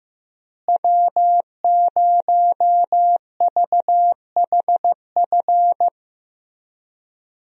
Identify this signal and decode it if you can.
Morse code, heard as a keyed tone, about 16 wpm, 700 Hz. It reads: W0VHF
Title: USA Call Signs - 5 Characters - Repeated 3x - 15wpm